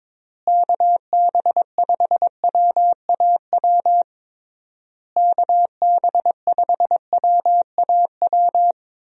Morse code, heard as a keyed tone, about 22 wpm, 700 Hz. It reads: KB5WAW KB5WAW
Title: USA Call Signs - 6 Characters - Repeated 3x - 22wpm